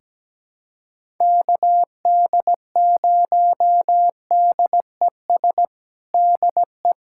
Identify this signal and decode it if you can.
Morse code, heard as a keyed tone, about 17 wpm, 700 Hz. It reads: KD0DES DE
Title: USA Callsign DE Callsign - 6 Characters - 17wpm